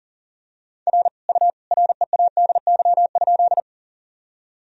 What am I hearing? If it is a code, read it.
Morse code, heard as a keyed tone, about 40 wpm, 700 Hz. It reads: R U READY?
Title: Sentences from Top 400 Words and Abbreviations 40wpm